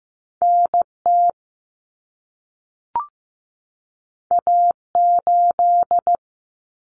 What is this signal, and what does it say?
Morse code, heard as a keyed tone, about 15 wpm, 700 Hz. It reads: NT A8
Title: Common Cut Numbers - 2-Digit 15wpm